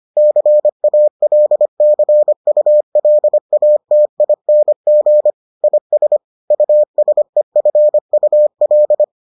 Morse code, heard as a keyed tone, about 25 wpm, 600 Hz. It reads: CALCULATING IS USEFUL